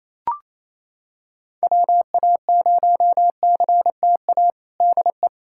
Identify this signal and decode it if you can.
Morse code, heard as a keyed tone, about 28 wpm, 700 Hz. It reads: WA0CTA DE
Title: USA Callsign DE Callsign - 6 Characters - 28wpm